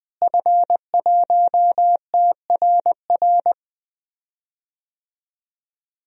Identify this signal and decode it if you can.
Morse code, heard as a keyed tone, about 20 wpm, 700 Hz. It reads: F1TRR
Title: DXCC-100 Call Signs - 5 Characters - Repeated 3x - 20wpm